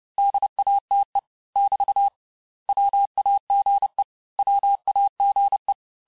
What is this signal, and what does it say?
Morse code, heard as a keyed tone, about 30 wpm, 800 Hz. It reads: DATE <BT> WAGE WAGE
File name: cours-du-mercredi / audios / cours-854-mots-vitesse-28-mpm